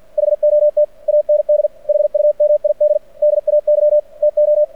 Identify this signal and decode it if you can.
Morse code, heard as a keyed tone, about 49 wpm, 600 Hz. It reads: HOT AND HUMID DAY TO
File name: archived / CRSnet